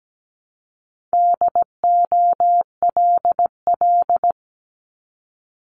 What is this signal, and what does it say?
Morse code, heard as a keyed tone, about 17 wpm, 700 Hz. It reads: DOLL
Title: Multi-letter Phonemes 17wpm